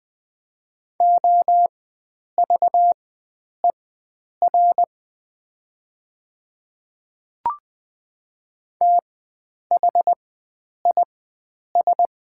Morse code, heard as a keyed tone, about 21 wpm, spaced as for 9 wpm, 700 Hz. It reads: OVER THIS
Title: 4x Character Spacing - Top 100 Words - 20wpm